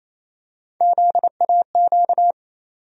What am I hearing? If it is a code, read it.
Morse code, heard as a keyed tone, about 28 wpm, 700 Hz. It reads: ZAQ
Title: USA Call Sign Postfixes - 3 Letters 28wpm